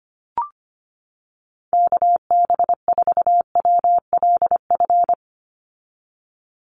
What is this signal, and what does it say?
Morse code, heard as a keyed tone, about 25 wpm, 700 Hz. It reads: KB4WLF
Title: Call Signs - 25wpm